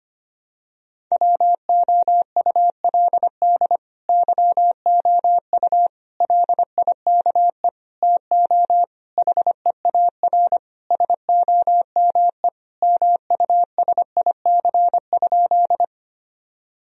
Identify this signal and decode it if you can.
Morse code, heard as a keyed tone, about 25 wpm, 700 Hz. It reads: WOULD YOU LIKE TO HEAR SOME MUSIC?
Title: Sentences from Top 1000 Words - Review Part 1 - 25wpm